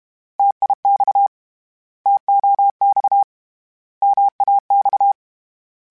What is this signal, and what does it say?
Morse code, heard as a keyed tone, about 32 wpm, 800 Hz. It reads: TIX TOX MAX
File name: Contrasts / Contrasts-mp3s / 3LtrContrasts @30